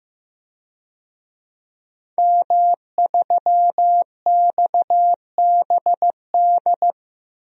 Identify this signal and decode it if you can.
Morse code, heard as a keyed tone, about 15 wpm, 700 Hz. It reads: M3XBD
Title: DXCC-100 Call Signs - 5 Characters - Repeated 3x - 15wpm